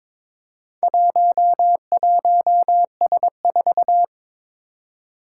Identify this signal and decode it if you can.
Morse code, heard as a keyed tone, about 22 wpm, 700 Hz. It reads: 11S4
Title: Four Challenging Letter-Number Combinations 22wpm